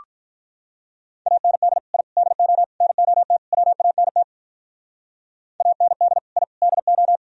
Morse code, heard as a keyed tone, about 44 wpm, 700 Hz. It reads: AND I DO NOT WANT AND I DO
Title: Speed-Racing - Sets of 5 Words 35wpm